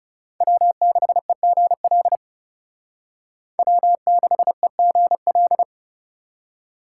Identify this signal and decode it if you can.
Morse code, heard as a keyed tone, about 38 wpm, 700 Hz. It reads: W6EGL W6EGL